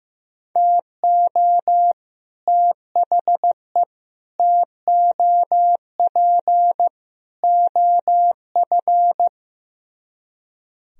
Morse code, heard as a keyed tone, about 15 wpm, 700 Hz. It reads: TO THE TOP OF
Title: Speed-Racing - Sets of 4 Words 15wpm